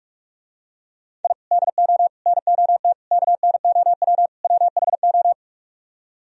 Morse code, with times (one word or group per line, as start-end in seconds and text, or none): 1.2-1.3 I
1.5-2.1 DO
2.3-2.9 NOT
3.1-4.3 KNOW
4.4-5.3 WHO